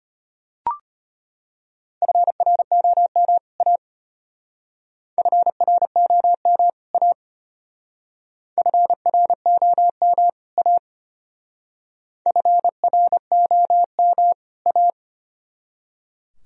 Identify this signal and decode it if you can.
Morse code, heard as a keyed tone, about 32 wpm, 700 Hz. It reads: FROM A FROM A FROM A FROM A